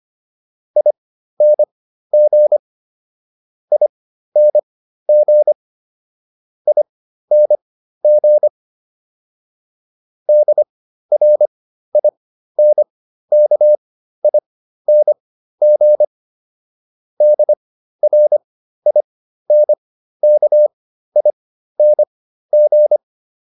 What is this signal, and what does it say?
Morse code, heard as a keyed tone, about 25 wpm, spaced as for 13 wpm, 600 Hz. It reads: ING ING ING DRINKING DRINKING